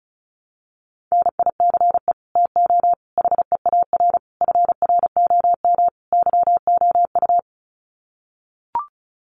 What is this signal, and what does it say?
Morse code, heard as a keyed tone, about 34 wpm, 700 Hz. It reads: NICE TO HEAR FROM YOU
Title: Common QSO Phrases 35wpm